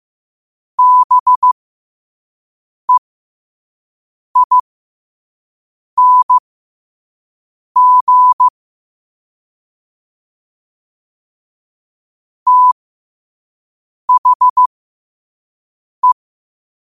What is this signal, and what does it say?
Morse code, heard as a keyed tone, about 15 wpm, spaced as for 5 wpm, 1000 Hz. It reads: BEING THE
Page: Quotes for Tue, 12 Aug 2025 in Morse Code at 5 words per minute.